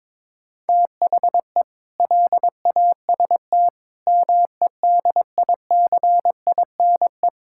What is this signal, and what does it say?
Morse code, heard as a keyed tone, about 22 wpm, 700 Hz. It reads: THE LAST MEDICINE
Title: Sentences from Top 1000 Words - Review Part 6 - 22wpm